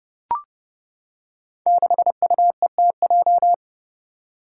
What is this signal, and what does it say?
Morse code, heard as a keyed tone, about 30 wpm, 700 Hz. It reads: 6UETJ
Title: Five Challenging Letter-Number Combinations 30wpm